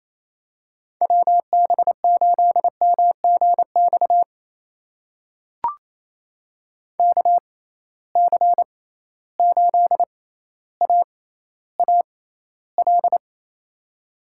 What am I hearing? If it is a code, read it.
Morse code, heard as a keyed tone, about 28 wpm, spaced as for 14 wpm, 700 Hz. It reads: WB8MGX KC8AAL
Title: USA Call Signs - 6 Characters - 6x Character Spacing - 28wpm